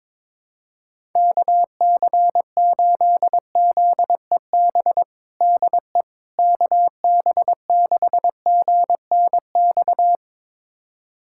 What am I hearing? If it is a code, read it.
Morse code, heard as a keyed tone, about 22 wpm, 700 Hz. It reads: KC8ZEB DE KB6GNX
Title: USA Callsign DE Callsign 22wpm